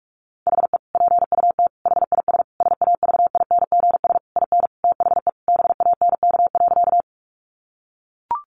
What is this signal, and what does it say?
Morse code, heard as a keyed tone, about 44 wpm, 700 Hz. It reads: HE PUT HIS SAVINGS IN THE BANK.
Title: Sentences from Top 1200 Words 45wpm